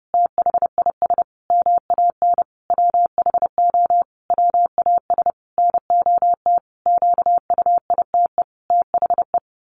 Morse code, heard as a keyed tone, about 30 wpm, 700 Hz. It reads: THIS MAN WHO WAS NOT QUITE THE
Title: A Sound Of Thunder 30wpm